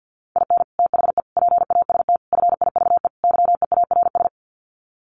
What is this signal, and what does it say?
Morse code, heard as a keyed tone, about 50 wpm, 700 Hz. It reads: IN THE PAST FIVE YEARS
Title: Sets of 5 Words Encore 50wpm